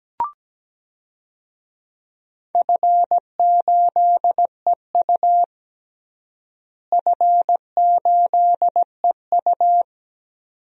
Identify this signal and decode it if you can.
Morse code, heard as a keyed tone, about 17 wpm, 700 Hz. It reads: F8EU F8EU
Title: DXCC-100 Call Signs - 4 Characters - Repeated 2x - 17wpm